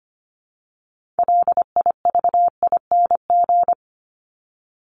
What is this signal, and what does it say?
Morse code, heard as a keyed tone, about 25 wpm, 700 Hz. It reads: LIVING
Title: ING Suffix - Top 100 Words with Base Words 25wpm